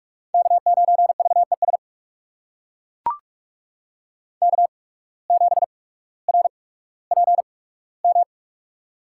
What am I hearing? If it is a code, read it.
Morse code, heard as a keyed tone, about 43 wpm, spaced as for 19 wpm, 700 Hz. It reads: K9VES K7RPM
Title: USA Call Signs - 5 Characters - 8x Character Spacing - 45wpm